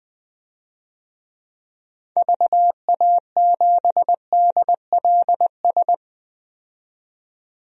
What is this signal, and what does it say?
Morse code, heard as a keyed tone, about 20 wpm, 700 Hz. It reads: VA7DLS
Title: DXCC-100 Call Signs - 6 Characters - 20wpm